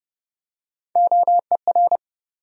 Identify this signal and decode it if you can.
Morse code, heard as a keyed tone, about 30 wpm, 700 Hz. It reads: OER
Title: USA Call Sign Postfixes - 3 Letters 30wpm